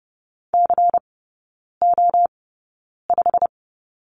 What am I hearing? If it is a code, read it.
Morse code, heard as a keyed tone, about 30 wpm, spaced as for 9 wpm, 690 Hz. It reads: CO5
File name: Lesson 16e - Letter c - Three Characters 30wpm